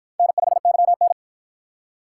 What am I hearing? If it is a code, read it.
Morse code, heard as a keyed tone, about 53 wpm, 700 Hz. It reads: N5YN